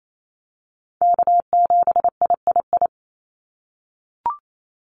Morse code, heard as a keyed tone, about 28 wpm, 700 Hz. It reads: K7III
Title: NA SOTA Chaser Callsigns 28wpm